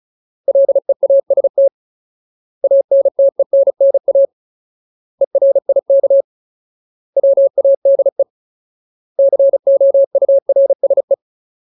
Morse code, heard as a keyed tone, about 36 wpm, 550 Hz. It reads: LEAST ANTENNA ERIK WADE COURSE
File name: Ultimate Word List / 35WPM Ultimate